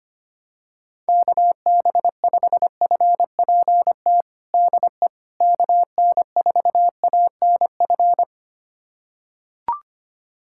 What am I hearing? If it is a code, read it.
Morse code, heard as a keyed tone, about 25 wpm, 700 Hz. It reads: KB5FPT DE KN4ANF
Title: USA Callsign DE Callsign - 6 Characters - 25wpm